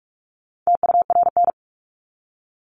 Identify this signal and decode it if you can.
Morse code, heard as a keyed tone, about 44 wpm, 700 Hz. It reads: TURN
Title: ED Suffix - Top 100 Words with Base Words 45wpm